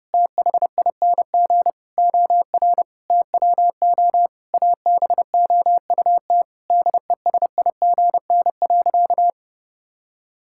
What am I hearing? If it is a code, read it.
Morse code, heard as a keyed tone, about 32 wpm, 700 Hz. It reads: THING OR TWO ABOUT DESIGN.
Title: Sentences from Top 1000 Words - Review Part 5 - 30wpm